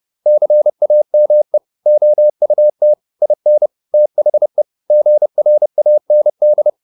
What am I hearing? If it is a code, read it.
Morse code, heard as a keyed tone, about 30 wpm, 600 Hz. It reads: CAME OUT IN THE GRAND
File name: the old man and the sea / the old man and the sea 1 30wpm